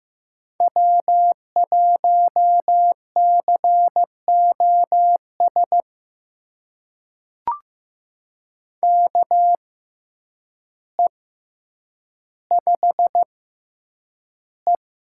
Morse code, heard as a keyed tone, about 15 wpm, spaced as for 8 wpm, 700 Hz. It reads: W1COS KE5E
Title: USA Call Signs - 6x Character Spacing - 15wpm